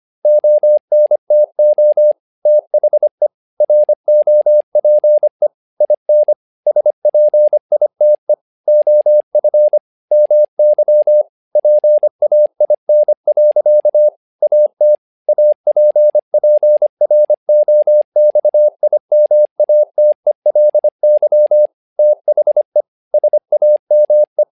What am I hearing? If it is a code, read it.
Morse code, heard as a keyed tone, about 25 wpm, 600 Hz. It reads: ONTO THE ROPE IN SPITE OF MY PAIN. AT APPROXIMATELY THE SAME
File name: learning center / tower trouble / tower trouble 25wpm